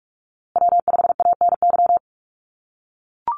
Morse code, heard as a keyed tone, about 44 wpm, 700 Hz. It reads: W5ANY
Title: USA Call Signs Repeated 3x 45wpm